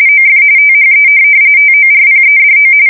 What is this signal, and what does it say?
RTTY, about 45.45 baud. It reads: OF THE TORPEDO
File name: fsk170 / USS-Missouri